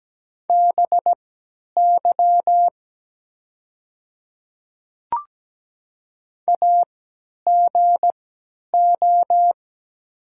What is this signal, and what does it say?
Morse code, heard as a keyed tone, about 17 wpm, spaced as for 10 wpm, 700 Hz. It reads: BY AGO
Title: 3x Character Spacing - Top 500 Words - 17wpm